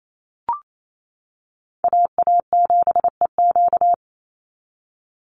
Morse code, heard as a keyed tone, about 28 wpm, 700 Hz. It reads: AA7EQ
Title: NA SOTA Chaser Callsigns 28wpm